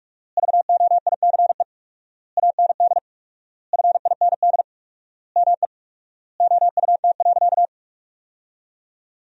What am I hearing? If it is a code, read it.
Morse code, heard as a keyed tone, about 44 wpm, 700 Hz. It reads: VOICE AND FIND ME OUT.
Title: 4x Word-Spacing - Sentences from Top 500 Words - 45wpm